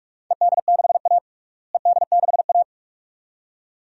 Morse code, heard as a keyed tone, about 46 wpm, 700 Hz. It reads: ED6A ED6A
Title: DXCC-100 Call Signs - 4 Characters - Repeated 2x - 45wpm